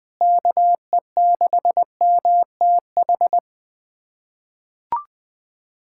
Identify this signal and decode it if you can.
Morse code, heard as a keyed tone, about 20 wpm, 700 Hz. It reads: KE6MTH
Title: USA Callsign DE Callsign - 6 Characters - 20wpm